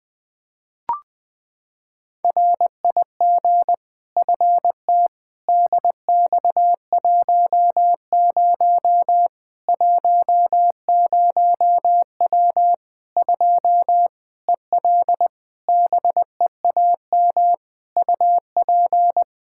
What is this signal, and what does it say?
Morse code, heard as a keyed tone, about 20 wpm, 700 Hz. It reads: RIG FT DX10 10W 2 EL BEAM UP
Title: K1USN SST QSO Practice 20wpm